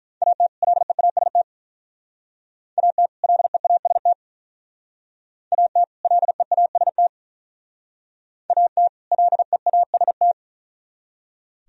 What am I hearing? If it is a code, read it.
Morse code, heard as a keyed tone, about 44 wpm, 700 Hz. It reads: AT LEAST AT LEAST AT LEAST AT LEAST